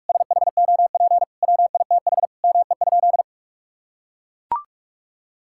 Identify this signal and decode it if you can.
Morse code, heard as a keyed tone, about 42 wpm, 700 Hz. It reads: SHOP WITH ME?
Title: Sentences from Top 1600 Words 45wpm